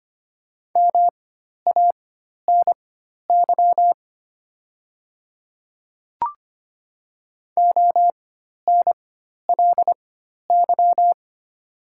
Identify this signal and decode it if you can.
Morse code, heard as a keyed tone, about 25 wpm, spaced as for 12 wpm, 700 Hz. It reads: MANY ONLY
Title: 4x Character Spacing - Top 500 Words - 25wpm